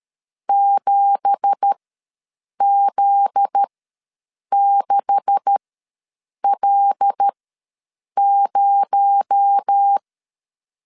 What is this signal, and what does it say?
Morse code, heard as a keyed tone, about 13 wpm, spaced as for 7 wpm, 800 Hz. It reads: 7Z6L0
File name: LEZIONI / 10 SIGNS / 10B INT ws 12wpm 3527